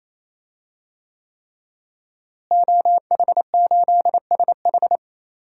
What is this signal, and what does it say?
Morse code, heard as a keyed tone, about 27 wpm, 700 Hz. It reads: OH8SH
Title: DXCC-100 Call Signs - 5 Characters - Repeated 3x - 28wpm